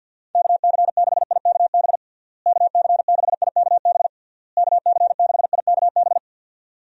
Morse code, heard as a keyed tone, about 52 wpm, 700 Hz. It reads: KC6IKB KC6IKB KC6IKB
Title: USA Call Signs - 6 Characters - Repeated 3x - 50wpm